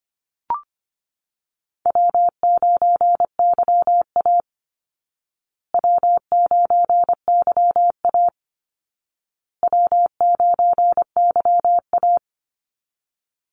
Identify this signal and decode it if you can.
Morse code, heard as a keyed tone, about 25 wpm, 700 Hz. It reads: W9YA W9YA W9YA
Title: Easy Call Signs Repeated 3x - 25wpm